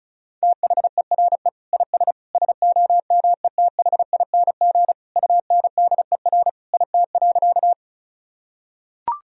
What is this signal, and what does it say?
Morse code, heard as a keyed tone, about 34 wpm, 700 Hz. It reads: THERE IS SOMETHING UNDER IT.
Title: Sentences from Top 1000 Words - Review Part 1 - 35wpm